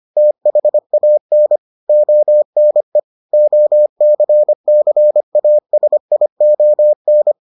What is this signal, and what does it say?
Morse code, heard as a keyed tone, about 25 wpm, 600 Hz. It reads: THAN ONE OCCASION